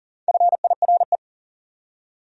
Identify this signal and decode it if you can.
Morse code, heard as a keyed tone, about 52 wpm, 700 Hz. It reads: FIRE